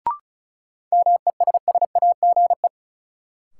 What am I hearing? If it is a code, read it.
Morse code, heard as a keyed tone, about 36 wpm, 700 Hz. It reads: MESSAGE
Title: Sentences from Top 1100 Words - New Word Drill Alt - 35wpm